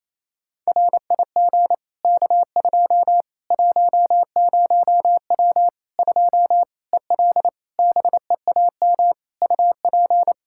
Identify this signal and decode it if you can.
Morse code, heard as a keyed tone, about 28 wpm, 700 Hz. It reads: RIG K2 10W 2 EL BEAM UP
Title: Practice QSO - Exchange 2 - Info - 28wpm